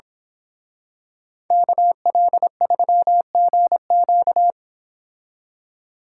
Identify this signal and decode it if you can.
Morse code, heard as a keyed tone, about 25 wpm, 700 Hz. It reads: KL3GQ
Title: Speed Racing - USA 2x2 Call Signs 22wpm